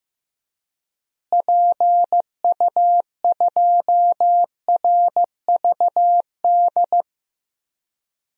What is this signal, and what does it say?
Morse code, heard as a keyed tone, about 15 wpm, 700 Hz. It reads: PU2RVD
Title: DXCC-100 Call Signs - 6 Characters - Repeated 2x - 15wpm